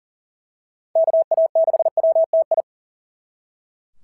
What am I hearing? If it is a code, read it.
Morse code, heard as a keyed tone, about 41 wpm, 690 Hz. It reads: KA6WTI
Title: SOTA QSO Practice 40wpm